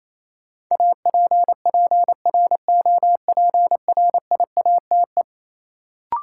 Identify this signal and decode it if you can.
Morse code, heard as a keyed tone, about 28 wpm, 700 Hz. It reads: APPROPRIATE
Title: Sentences from Top 900 Words - New Word Drill - 28wpm